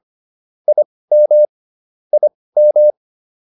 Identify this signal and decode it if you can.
Morse code, heard as a keyed tone, about 25 wpm, spaced as for 18 wpm, 600 Hz. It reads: IM IM